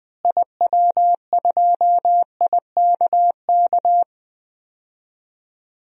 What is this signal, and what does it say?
Morse code, heard as a keyed tone, about 21 wpm, 700 Hz. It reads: IW2IKK
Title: DXCC-100 Call Signs - 6 Characters - Repeated 2x - 20wpm